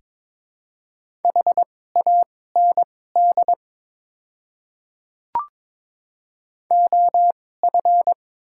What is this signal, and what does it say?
Morse code, heard as a keyed tone, about 22 wpm, spaced as for 16 wpm, 700 Hz. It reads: HAND OF
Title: 2x Character Spacing - Top 300 Words - 22wpm